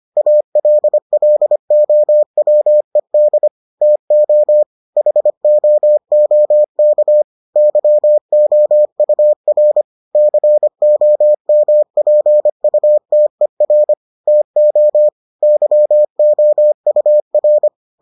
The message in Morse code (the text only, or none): ALLOWED TO HOOK YOUR COMPUTER TO YOUR